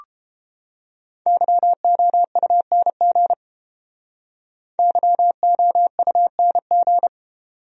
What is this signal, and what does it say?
Morse code, heard as a keyed tone, about 28 wpm, 700 Hz. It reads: YOUNG YOUNG